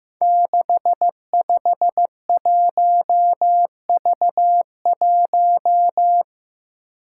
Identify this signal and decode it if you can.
Morse code, heard as a keyed tone, about 15 wpm, 700 Hz. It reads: 651V1
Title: Five Challenging Letter-Number Combinations 15wpm